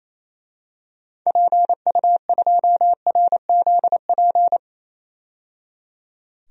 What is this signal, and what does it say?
Morse code, heard as a keyed tone, about 28 wpm, 700 Hz. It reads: PU2RZP
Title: DXCC-100 Call Signs - 6 Characters - 28wpm